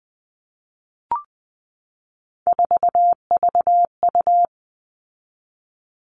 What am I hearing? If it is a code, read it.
Morse code, heard as a keyed tone, about 20 wpm, 700 Hz. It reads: 4VU
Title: All Cut Numbers - 3-Digit 20wpm